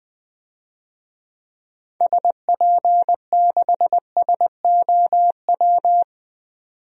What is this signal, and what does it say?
Morse code, heard as a keyed tone, about 20 wpm, 700 Hz. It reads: SP6SOW
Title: DXCC-100 Call Signs - 6 Characters - 20wpm